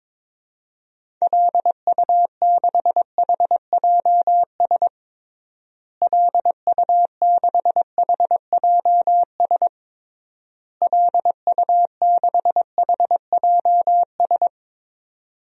Morse code, heard as a keyed tone, about 22 wpm, 700 Hz. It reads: LU6HJS LU6HJS LU6HJS
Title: DXCC-100 Call Signs - 6 Characters - Repeated 3x - 22wpm